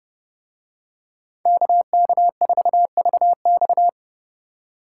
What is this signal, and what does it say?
Morse code, heard as a keyed tone, about 30 wpm, 700 Hz. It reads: KK4VX